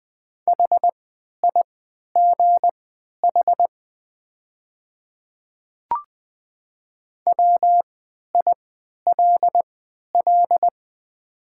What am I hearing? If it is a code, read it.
Morse code, heard as a keyed tone, about 20 wpm, spaced as for 11 wpm, 700 Hz. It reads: HIGH WILL
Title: 3x Character Spacing - Top 200 Words - 20wpm